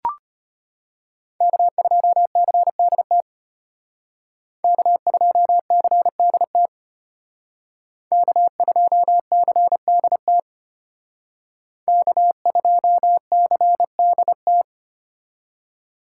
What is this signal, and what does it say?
Morse code, heard as a keyed tone, about 32 wpm, 700 Hz. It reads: K2CDT K2CDT K2CDT K2CDT